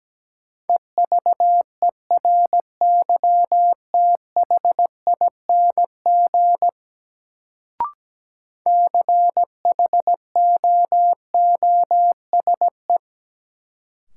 Morse code, heard as a keyed tone, about 17 wpm, 700 Hz. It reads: EVERYTHING CHOOSE
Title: Sentences from Top 700 Words - New Word Drill - 17wpm